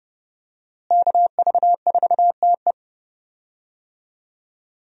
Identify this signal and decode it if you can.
Morse code, heard as a keyed tone, about 30 wpm, 700 Hz. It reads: KV4TE
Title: USA Call Signs - 5 Characters - Repeated 2x - 30wpm